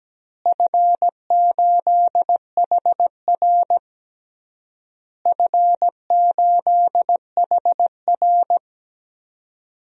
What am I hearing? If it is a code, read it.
Morse code, heard as a keyed tone, about 17 wpm, 700 Hz. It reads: F8HR F8HR
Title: DXCC-100 Call Signs - 4 Characters - Repeated 3x - 17wpm